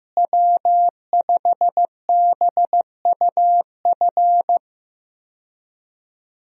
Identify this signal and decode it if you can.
Morse code, heard as a keyed tone, about 16 wpm, 700 Hz. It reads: W5BUF
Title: USA Call Signs - 5 Characters - Repeated 3x - 15wpm